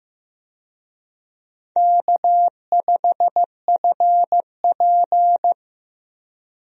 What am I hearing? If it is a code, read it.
Morse code, heard as a keyed tone, about 15 wpm, 700 Hz. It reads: K5FP
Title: USA Call Signs - 4 Characters - 15wpm